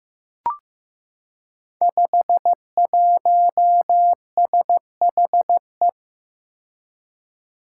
Morse code, heard as a keyed tone, about 15 wpm, 700 Hz. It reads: 51SHE
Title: Five Challenging Letter-Number Combinations 15wpm